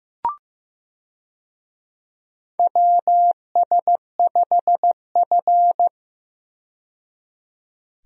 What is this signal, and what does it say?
Morse code, heard as a keyed tone, about 15 wpm, 700 Hz. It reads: WS5F
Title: USA Call Signs - 4 Characters - 15wpm